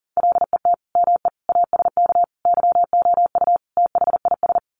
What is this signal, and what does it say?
Morse code, heard as a keyed tone, about 40 wpm, 700 Hz. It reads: LET ME ASK YOU THIS
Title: Sets of 5 Words 40wpm